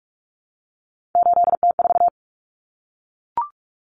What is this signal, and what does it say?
Morse code, heard as a keyed tone, about 44 wpm, 700 Hz. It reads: 8T4
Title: Common Cut Numbers - 3-Digit 45wpm